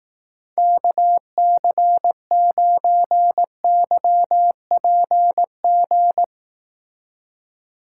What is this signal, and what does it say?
Morse code, heard as a keyed tone, about 18 wpm, 700 Hz. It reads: KC9YPG
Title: Speed Racing - USA 2x3 Call Signs 15wpm